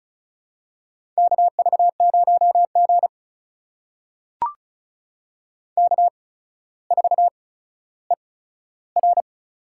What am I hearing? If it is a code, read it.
Morse code, heard as a keyed tone, about 36 wpm, spaced as for 15 wpm, 700 Hz. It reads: KV0G K4ER
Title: USA Call Signs - 4 Characters - 8x Character Spacing - 35wpm